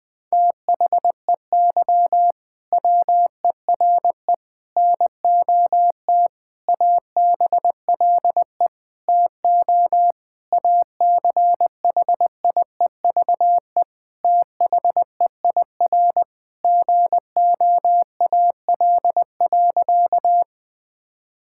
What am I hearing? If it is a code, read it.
Morse code, heard as a keyed tone, about 20 wpm, 700 Hz. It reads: THEY WERE NOT ABLE TO ACHIEVE THEIR GOAL.
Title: Sentences from Top 1000 Words - Review Part 6 - 20wpm